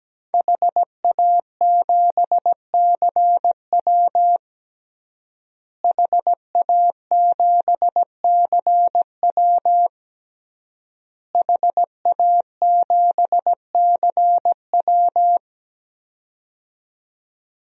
Morse code, heard as a keyed tone, about 17 wpm, 700 Hz. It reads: HA7CW HA7CW HA7CW
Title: DXCC-100 Call Signs - 5 Characters - Repeated 3x - 17wpm